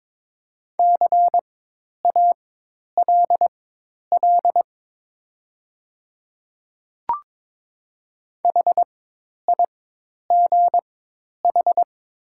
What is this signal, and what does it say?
Morse code, heard as a keyed tone, about 22 wpm, spaced as for 10 wpm, 700 Hz. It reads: CALL HIGH
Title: 4x Character Spacing - Top 300 Words - 22wpm